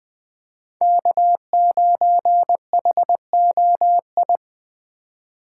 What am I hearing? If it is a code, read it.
Morse code, heard as a keyed tone, about 20 wpm, 700 Hz. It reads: K9HOI
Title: USA Call Signs - 5 Characters - 4x Character Spacing - 20wpm